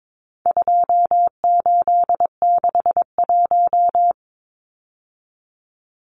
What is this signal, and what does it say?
Morse code, heard as a keyed tone, about 22 wpm, 700 Hz. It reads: 2861
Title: Numbers - 4-Digit 22wpm